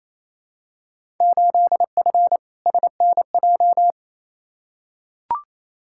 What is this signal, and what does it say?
Morse code, heard as a keyed tone, about 28 wpm, 700 Hz. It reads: 8F SNJ
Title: Field Day Exchanges Encore - 28wpm